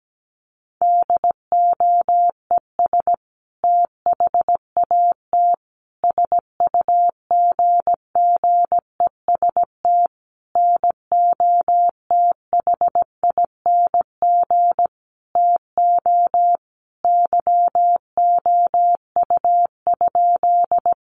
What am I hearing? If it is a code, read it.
Morse code, heard as a keyed tone, about 17 wpm, 700 Hz. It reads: DOES THAT SUGGEST NOTHING TO YOU?
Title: Sentences from Top 1300 Words 17wpm